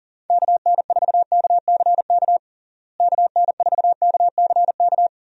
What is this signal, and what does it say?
Morse code, heard as a keyed tone, about 37 wpm, 700 Hz. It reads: KN4KCK KN4KCK
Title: USA Call Signs - 6 Characters - Repeated 2x - 40wpm